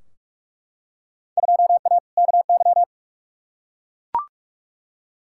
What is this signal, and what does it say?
Morse code, heard as a keyed tone, about 42 wpm, 700 Hz. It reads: 2A KY
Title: Field Day Exchanges - 45wpm